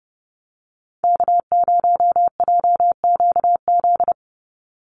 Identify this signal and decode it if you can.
Morse code, heard as a keyed tone, about 30 wpm, 700 Hz. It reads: K0JQZ
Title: NA SOTA Activator Callsigns 30wpm